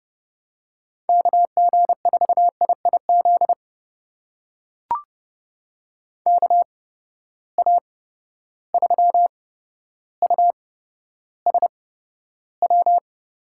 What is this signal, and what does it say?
Morse code, heard as a keyed tone, about 30 wpm, spaced as for 13 wpm, 700 Hz. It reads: KG4IIZ KA3USW
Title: USA Call Signs - 6 Characters - 8x Character Spacing - 30wpm